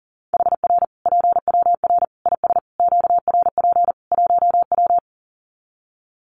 Morse code, heard as a keyed tone, about 40 wpm, 700 Hz. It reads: HR PWR IS QRP 1W
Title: Common QSO Phrases 40wpm